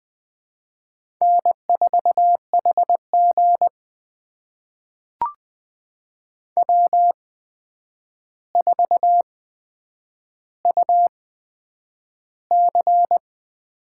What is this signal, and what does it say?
Morse code, heard as a keyed tone, about 20 wpm, spaced as for 9 wpm, 700 Hz. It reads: N4HG W4UC
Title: USA Call Signs - 4 Characters - 8x Character Spacing - 20wpm